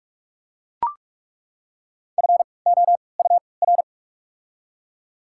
Morse code, heard as a keyed tone, about 41 wpm, spaced as for 25 wpm, 690 Hz. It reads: FOUR
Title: 3x Character Spacing - Top 300 Words - 45wpm